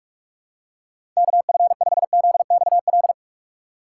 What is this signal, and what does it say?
Morse code, heard as a keyed tone, about 46 wpm, 700 Hz. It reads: KF5ZXL
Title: USA Call Signs - 6 Characters - 3x Character Spacing - 45wpm